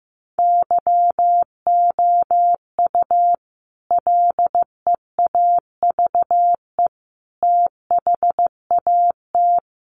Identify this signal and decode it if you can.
Morse code, heard as a keyed tone, about 15 wpm, 700 Hz. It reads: YOU LEAVE THAT
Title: Sentences from Top 1400 Words 15wpm